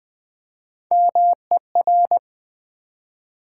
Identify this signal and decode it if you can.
Morse code, heard as a keyed tone, about 20 wpm, 700 Hz. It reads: MER
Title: Build A Word - 1 Letter At A Time - 5-Letter Words - 20wpm